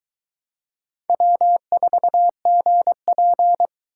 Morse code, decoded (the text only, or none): W4GP